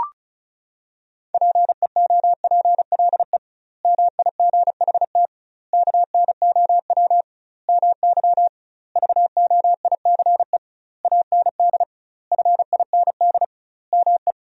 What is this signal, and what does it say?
Morse code, heard as a keyed tone, about 34 wpm, 700 Hz. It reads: PEOPLE MIGHT KNOW MY VOICE AND FIND ME
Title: 2x Word-Spacing - Sentences from Top 500 Words - 35wpm